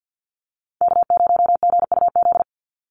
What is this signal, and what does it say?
Morse code, heard as a keyed tone, about 50 wpm, 700 Hz. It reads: K0GUZ
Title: Call Signs - 50wpm